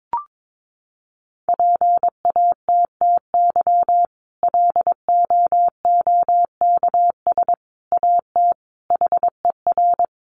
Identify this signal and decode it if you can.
Morse code, heard as a keyed tone, about 22 wpm, 700 Hz. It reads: PATTY LOOKS AT HER
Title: Additional Easy Phrases 22wpm